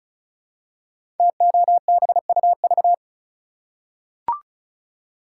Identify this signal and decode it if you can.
Morse code, heard as a keyed tone, about 36 wpm, 700 Hz. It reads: TOBUV
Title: Code Groups - 5 Letters - 35wpm